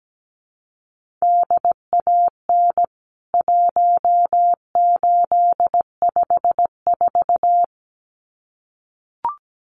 Morse code, heard as a keyed tone, about 17 wpm, 700 Hz. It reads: DAN 1854
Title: CWT Exchange 17wpm